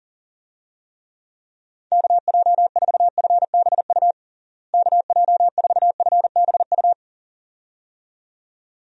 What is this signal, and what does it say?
Morse code, heard as a keyed tone, about 40 wpm, 700 Hz. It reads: KJ4FBU KJ4FBU
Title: USA Call Signs - 6 Characters - Repeated 2x - 40wpm